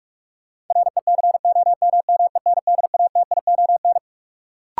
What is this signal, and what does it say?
Morse code, heard as a keyed tone, about 43 wpm, 700 Hz. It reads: RECOMMENDATION
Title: Sentences from Top 1400 Words - New Word Drill - 45wpm